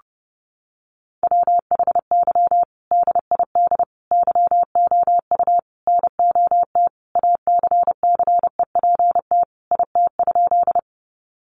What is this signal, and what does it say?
Morse code, heard as a keyed tone, about 30 wpm, 700 Hz. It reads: WHY DID YOU NOT ACCEPT IT?
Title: Sentences from Top 1300 Words 30wpm